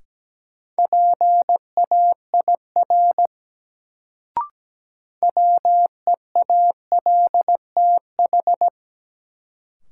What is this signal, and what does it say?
Morse code, heard as a keyed tone, about 17 wpm, 700 Hz. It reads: PAIR WEALTH
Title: Sentences from Top 1200 Words - New Word Drill Alt - 17wpm